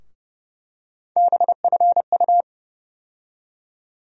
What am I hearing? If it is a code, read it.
Morse code, heard as a keyed tone, about 30 wpm, 700 Hz. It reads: BFU
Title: USA Call Sign Postfixes - 3 Letters 30wpm